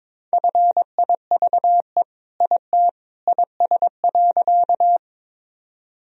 Morse code, heard as a keyed tone, about 22 wpm, 700 Hz. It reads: FIVE IT IS.
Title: Sentences from Top 1000 Words - Review Part 2 - 22wpm